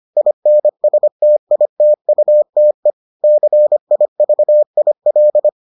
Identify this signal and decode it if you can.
Morse code, heard as a keyed tone, about 25 wpm, 600 Hz. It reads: INSTITUTE CIVIL